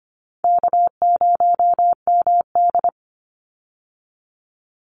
Morse code, heard as a keyed tone, about 25 wpm, 700 Hz. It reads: K0MD
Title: Call Signs - 25wpm